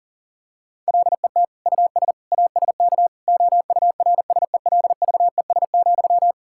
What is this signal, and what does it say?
Morse code, heard as a keyed tone, about 40 wpm, 700 Hz. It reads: LET US ASK OURSELVES,
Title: Sentences from Top 1600 Words 40wpm